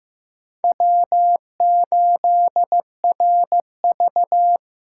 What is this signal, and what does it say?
Morse code, heard as a keyed tone, about 15 wpm, 700 Hz. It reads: W8RV
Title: USA Callsign DE Callsign - 4 Characters - 15wpm